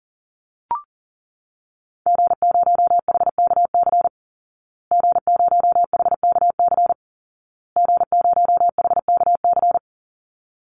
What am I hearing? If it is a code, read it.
Morse code, heard as a keyed tone, about 40 wpm, 700 Hz. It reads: G0HKC G0HKC G0HKC
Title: Call Signs Repeated 2x - 40wpm